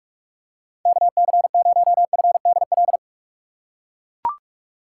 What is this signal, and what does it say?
Morse code, heard as a keyed tone, about 44 wpm, 690 Hz. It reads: KC0FDL
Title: USA Call Signs - 4x Character Spacing - 45wpm